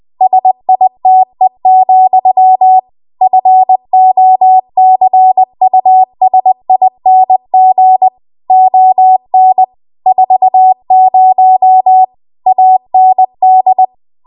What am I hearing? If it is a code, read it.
Morse code, heard as a keyed tone, about 20 wpm, 750 Hz. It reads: SITE, FOCUSING ON 40 AND
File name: Archive / 20 WPM / 20WPM